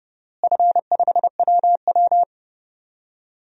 Morse code, heard as a keyed tone, about 30 wpm, 700 Hz. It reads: F5WW
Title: DXCC-100 Call Signs - 4 Characters - Repeated 3x - 30wpm